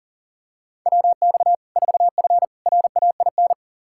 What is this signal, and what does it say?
Morse code, heard as a keyed tone, about 40 wpm, 700 Hz. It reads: WX 4F RAIN
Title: Practice QSO - Exchange 2 - Info - 40wpm